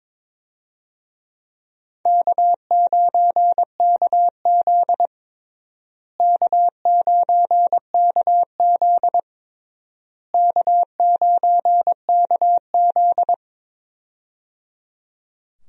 Morse code, heard as a keyed tone, about 22 wpm, 700 Hz. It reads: K9KZ K9KZ K9KZ
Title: USA Call Signs - 4 Characters - Repeated 3x - 22wpm